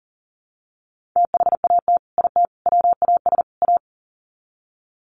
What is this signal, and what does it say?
Morse code, heard as a keyed tone, about 40 wpm, 700 Hz. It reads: THAT IT WAS A
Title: Sets of 4 Words 40wpm